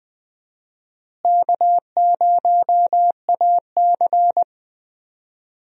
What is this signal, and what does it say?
Morse code, heard as a keyed tone, about 20 wpm, 700 Hz. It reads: K0AC
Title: USA Call Signs - 4 Characters - Repeated 2x - 20wpm